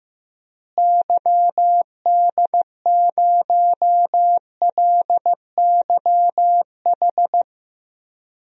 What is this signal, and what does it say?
Morse code, heard as a keyed tone, about 16 wpm, 700 Hz. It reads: YD0LYH
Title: DXCC-100 Call Signs - 6 Characters - Repeated 2x - 15wpm